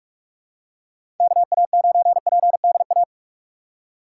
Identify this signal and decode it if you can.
Morse code, heard as a keyed tone, about 46 wpm, 700 Hz. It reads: KA9PDA
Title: USA Call Signs - 6 Characters - 4x Character Spacing - 45wpm